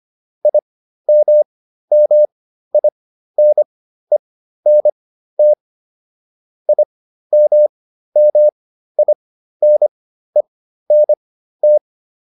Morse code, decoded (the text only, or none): IMMINENT IMMINENT